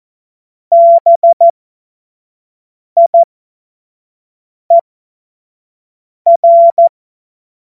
Morse code, heard as a keyed tone, about 14 wpm, spaced as for 5 wpm, 680 Hz. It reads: BIER